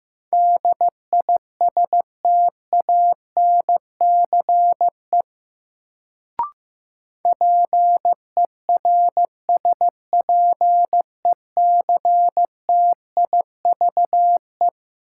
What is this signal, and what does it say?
Morse code, heard as a keyed tone, about 15 wpm, 700 Hz. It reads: DISTANCE PERSPECTIVE
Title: Sentences from Top 1200 Words - New Word Drill - 15wpm